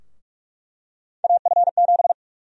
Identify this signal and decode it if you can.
Morse code, heard as a keyed tone, about 46 wpm, 700 Hz. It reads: AF7
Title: USA Call Sign Prefixes - Repeated 2x - 45wpm